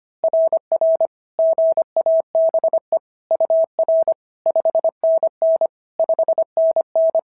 Morse code, heard as a keyed tone, about 22 wpm, 650 Hz. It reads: RR GABE UR 5NN 5NN
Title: SOTA QSO Practice 25wpm